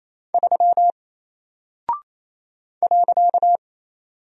Morse code, heard as a keyed tone, about 28 wpm, 700 Hz. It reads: 3 .
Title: Single Character (Letter:Number:Punctuation:Prosign) Alt 28wpm